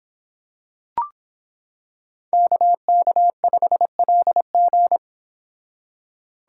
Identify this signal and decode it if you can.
Morse code, heard as a keyed tone, about 26 wpm, 700 Hz. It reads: KK5LG